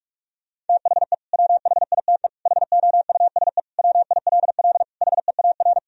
Morse code, heard as a keyed tone, about 41 wpm, 700 Hz. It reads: THE WHITE HOUSE WILL HEAR
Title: Sentences from Top 1000 Words - Review Part 1 - 45wpm